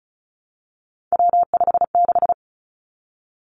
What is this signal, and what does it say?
Morse code, heard as a keyed tone, about 36 wpm, 700 Hz. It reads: W56
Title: Three Challenging Letter-Number Combinations 35wpm